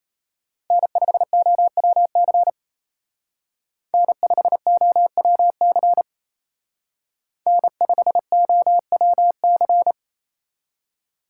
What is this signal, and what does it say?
Morse code, heard as a keyed tone, about 36 wpm, 700 Hz. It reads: N5OWC N5OWC N5OWC